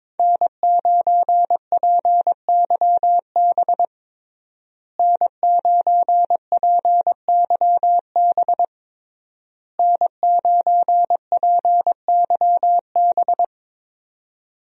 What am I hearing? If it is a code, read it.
Morse code, heard as a keyed tone, about 22 wpm, 700 Hz. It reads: N9PYB N9PYB N9PYB
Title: USA Call Signs - 5 Characters - Repeated 3x - 22wpm